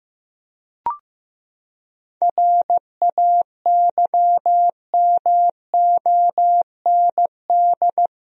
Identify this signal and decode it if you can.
Morse code, heard as a keyed tone, about 15 wpm, 700 Hz. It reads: RAYMOND
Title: USA Common Names - 15wpm